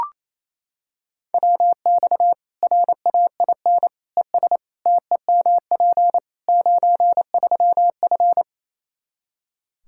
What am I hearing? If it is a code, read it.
Morse code, heard as a keyed tone, about 28 wpm, 700 Hz. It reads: WX RAIN ES TEMP 93F
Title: Practice QSO - Exchange 2 - Info - 28wpm